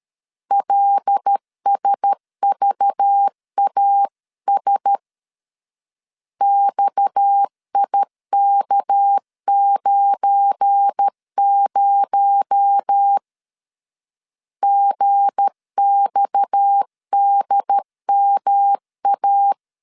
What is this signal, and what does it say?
Morse code, heard as a keyed tone, about 13 wpm, 800 Hz. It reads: LSVAS XIK90 GXDMA
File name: LEZIONI / 8 YJ45 / 8 YJ 12wpm 2235